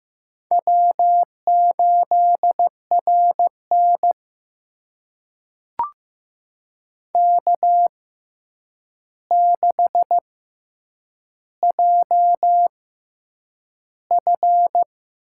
Morse code, heard as a keyed tone, about 15 wpm, spaced as for 8 wpm, 700 Hz. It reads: W8RN K6JF
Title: USA Call Signs - 4 Characters - 6x Character Spacing - 15wpm